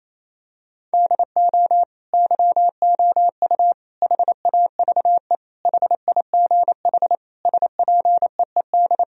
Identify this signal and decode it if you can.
Morse code, heard as a keyed tone, about 28 wpm, 700 Hz. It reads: DO YOU HAVE HIGH SPEED
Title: Sentences from Top 1000 Words - Review Part 4 - 28wpm